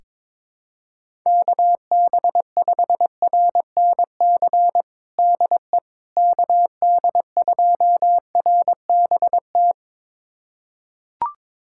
Morse code, heard as a keyed tone, about 22 wpm, 700 Hz. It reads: KB5RNC DE KD2RBT
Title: USA Callsign DE Callsign - 6 Characters - 22wpm